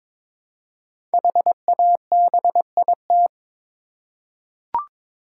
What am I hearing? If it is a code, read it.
Morse code, heard as a keyed tone, about 22 wpm, 700 Hz. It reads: HABIT
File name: 5 Letter Words - First 3 Letters Given - 22wpm